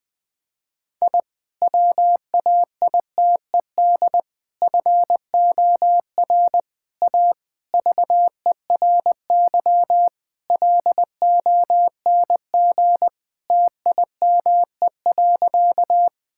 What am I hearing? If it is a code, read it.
Morse code, heard as a keyed tone, about 20 wpm, 700 Hz. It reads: I WAITED FOR A VERY LONG TIME.
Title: Sentences with 5-Word Phrases - 20wpm